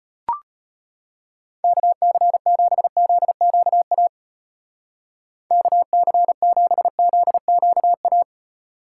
Morse code, heard as a keyed tone, about 32 wpm, 700 Hz. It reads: KC7ZQA KC7ZQA